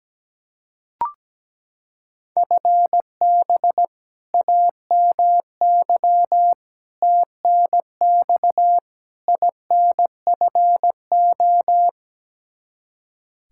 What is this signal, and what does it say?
Morse code, heard as a keyed tone, about 17 wpm, 700 Hz. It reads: FB AMY TNX INFO
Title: Practice QSO - Exchange 3 - More Info - 17wpm